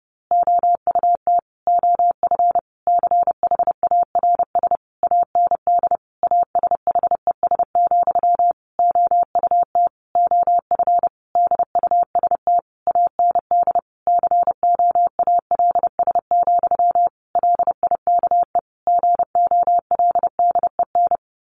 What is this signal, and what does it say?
Morse code, heard as a keyed tone, about 30 wpm, 700 Hz. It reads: OUT OF CHARS AND ASHES, OUT OF DUST AND COALS, LIKE GOLDEN
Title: A Sound Of Thunder 30wpm